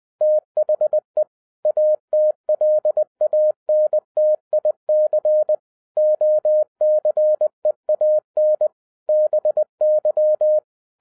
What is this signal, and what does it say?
Morse code, heard as a keyed tone, about 20 wpm, 610 Hz. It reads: THE ATLANTIC OCEAN BY